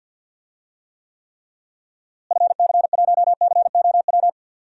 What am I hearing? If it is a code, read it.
Morse code, heard as a keyed tone, about 51 wpm, 700 Hz. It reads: FC1COW
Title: DXCC-100 Call Signs - 6 Characters - Repeated 3x - 50wpm